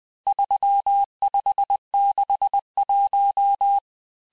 Morse code, heard as a keyed tone, about 20 wpm, 800 Hz. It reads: 3561
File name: cours-858-texte-chiffre-vitesse-19mpm